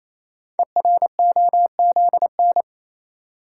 Morse code, heard as a keyed tone, about 29 wpm, 700 Hz. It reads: EROZN
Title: Code Groups - 5 Letters - 28wpm